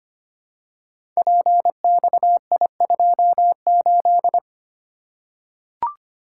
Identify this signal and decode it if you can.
Morse code, heard as a keyed tone, about 25 wpm, 700 Hz. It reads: PXI28
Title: Code Groups - 5 Letter:Numbers - 25wpm